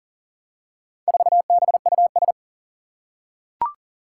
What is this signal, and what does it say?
Morse code, heard as a keyed tone, about 38 wpm, 700 Hz. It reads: 4BUS
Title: Four Challenging Letter-Number Combinations 40wpm